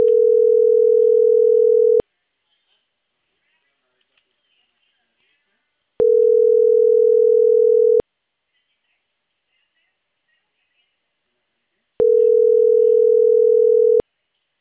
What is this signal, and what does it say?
Ringback tone (North America), 440 + 480 Hz.